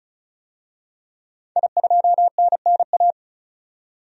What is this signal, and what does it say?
Morse code, heard as a keyed tone, about 36 wpm, 700 Hz. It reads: I2NNA
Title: DXCC-100 Call Signs - 5 Characters - 35wpm